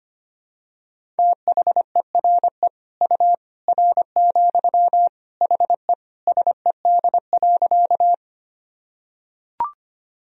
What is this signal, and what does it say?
Morse code, heard as a keyed tone, about 25 wpm, 700 Hz. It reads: THERE U R, HE SED.
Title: Sentences from Top 100 Words and Abbreviations 25wpm